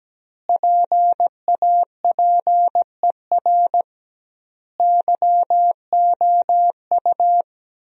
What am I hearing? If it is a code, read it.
Morse code, heard as a keyed tone, about 17 wpm, 700 Hz. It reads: PAPER YOU
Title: 2x Word-Spacing - Sentences from Top 400 Words - 17wpm